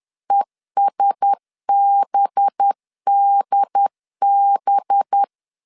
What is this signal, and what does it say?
Morse code, heard as a keyed tone, about 10 wpm, 800 Hz. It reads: ESBDB
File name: CW / DET / 1 6 1500C S B 10wpm 4137